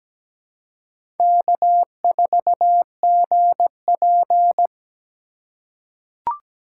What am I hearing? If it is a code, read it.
Morse code, heard as a keyed tone, about 17 wpm, 700 Hz. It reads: K4GP
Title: USA Call Signs - 4 Characters - 4x Character Spacing - 17wpm